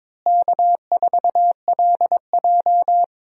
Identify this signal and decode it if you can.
Morse code, heard as a keyed tone, about 23 wpm, 700 Hz. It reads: K4LJ
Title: USA Call Signs - 4 Characters - Repeated 3x - 22wpm